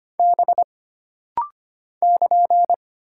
Morse code, heard as a keyed tone, about 25 wpm, 700 Hz. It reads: B <KN>
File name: Single Character (Letter:Number:Punctuation:Prosign) - Rapid-Fire - 25wpm